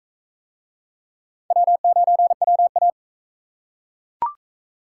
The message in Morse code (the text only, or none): W9WA